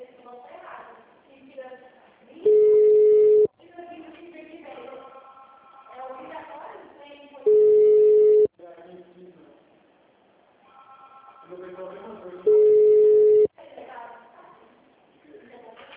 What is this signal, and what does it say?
Ringback tone (Europe, 425 Hz), with speech.